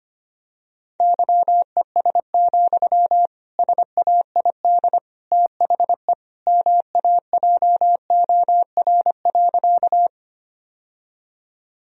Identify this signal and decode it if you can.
Morse code, heard as a keyed tone, about 25 wpm, 700 Hz. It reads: YES, SAID THE MAJOR.
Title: Sentences from Top 1000 Words - Review Part 4 - 25wpm